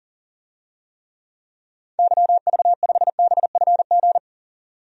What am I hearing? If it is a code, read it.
Morse code, heard as a keyed tone, about 40 wpm, 700 Hz. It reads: YV5BFG
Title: DXCC-100 Call Signs - 6 Characters - 40wpm